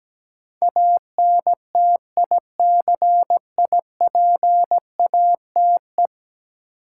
Morse code, heard as a keyed tone, about 17 wpm, 700 Hz. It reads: ANTICIPATE
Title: Sentences from Top 1500 Words - New Word Drill - 17wpm